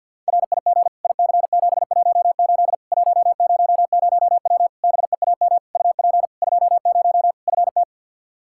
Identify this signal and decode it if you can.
Morse code, heard as a keyed tone, about 52 wpm, 700 Hz. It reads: RIG IC718 100W BEAM UP 20 FT
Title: K1USN SST QSO Practice 50wpm